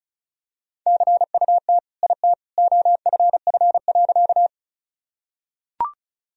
Morse code, heard as a keyed tone, about 36 wpm, 700 Hz. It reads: CUT IT OFF.
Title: Sentences from Top 1000 Words - Review Part 1 - 35wpm